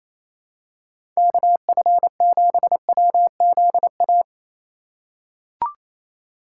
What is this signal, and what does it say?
Morse code, heard as a keyed tone, about 28 wpm, 700 Hz. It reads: KF7WZA
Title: USA Call Signs - 6 Characters - 4x Character Spacing - 28wpm